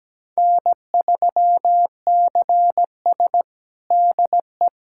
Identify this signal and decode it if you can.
Morse code, heard as a keyed tone, about 17 wpm, 700 Hz. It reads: N3CS DE
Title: USA Callsign DE Callsign - 4 Characters - 17wpm